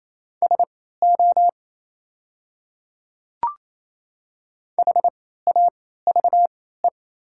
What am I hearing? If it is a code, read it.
Morse code, heard as a keyed tone, about 28 wpm, spaced as for 16 wpm, 690 Hz. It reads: SO HAVE
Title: 3x Character Spacing - Top 100 Words - 28wpm